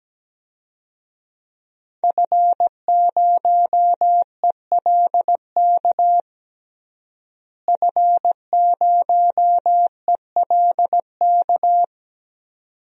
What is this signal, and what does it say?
Morse code, heard as a keyed tone, about 17 wpm, 700 Hz. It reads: F0ELK F0ELK
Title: DXCC-100 Call Signs - 5 Characters - Repeated 2x - 17wpm